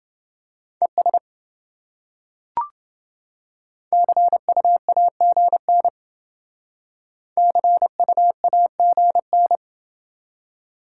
Morse code, heard as a keyed tone, about 28 wpm, 700 Hz. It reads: ES CUAGN CUAGN